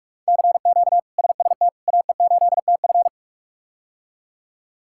Morse code, heard as a keyed tone, about 46 wpm, 700 Hz. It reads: CQ SST AE8TF
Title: K1USN SST QSO Practice - 45wpm